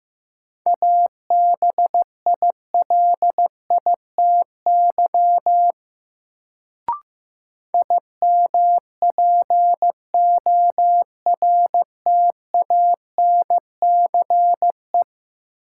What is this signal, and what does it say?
Morse code, heard as a keyed tone, about 15 wpm, 700 Hz. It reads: ABILITY IMPORTANCE
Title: Sentences from Top 1200 Words - New Word Drill Alt - 15wpm